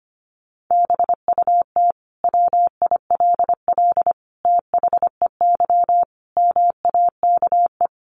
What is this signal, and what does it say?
Morse code, heard as a keyed tone, about 25 wpm, 700 Hz. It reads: BUT WILL THEY MAKE
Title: Sentences from Top 1000 Words 25wpm